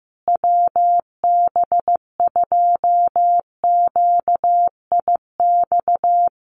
Morse code, heard as a keyed tone, about 15 wpm, 700 Hz. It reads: WB2QIX
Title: Common QSO Phrases Encore 15wpm